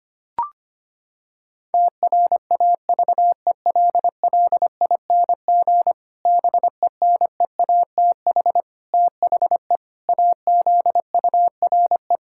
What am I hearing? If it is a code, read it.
Morse code, heard as a keyed tone, about 25 wpm, 700 Hz. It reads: TRAVELLING BENEATH THE AZURE
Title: Pangram Challenge v2 - 25wpm